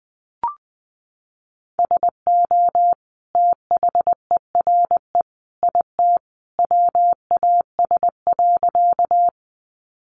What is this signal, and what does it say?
Morse code, heard as a keyed tone, about 20 wpm, 700 Hz. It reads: SO THERE IT WAS.
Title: Sentences from Top 100 Words 20wpm